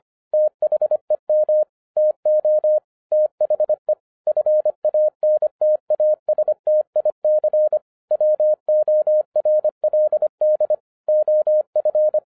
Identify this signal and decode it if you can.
Morse code, heard as a keyed tone, about 25 wpm, 610 Hz. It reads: THEM TO THE FANTASTIC WORLD OF